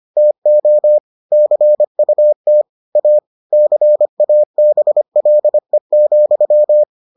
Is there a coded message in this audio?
Yes: Morse code from 0.2 to 6.8 s, heard as a keyed tone, about 25 wpm, 600 Hz. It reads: TO CUT A CABLE,